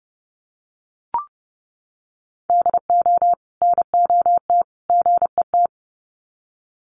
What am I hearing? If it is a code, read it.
Morse code, heard as a keyed tone, about 30 wpm, 700 Hz. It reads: DO NOT GET
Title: Sets of 3 Words 30wpm